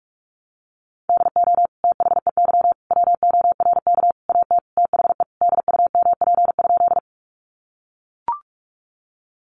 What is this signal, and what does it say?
Morse code, heard as a keyed tone, about 44 wpm, 700 Hz. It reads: DO THEY WORK AT THE DUMP?
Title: Sentences from Top 1300 Words 45wpm